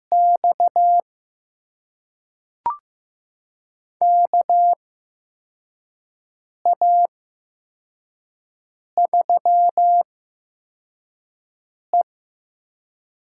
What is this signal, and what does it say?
Morse code, heard as a keyed tone, about 15 wpm, spaced as for 4 wpm, 700 Hz. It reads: X KA3E
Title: USA Call Signs - 4 Characters - 8x Character Spacing - 15wpm